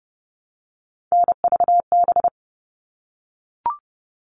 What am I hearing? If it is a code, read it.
Morse code, heard as a keyed tone, about 30 wpm, 700 Hz. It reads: NVB
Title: All Cut Numbers - 3-Digit 30wpm